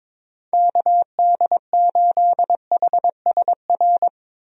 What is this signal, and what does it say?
Morse code, heard as a keyed tone, about 22 wpm, 700 Hz. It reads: KD8HSR
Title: USA Call Signs - 6 Characters - Repeated 2x - 22wpm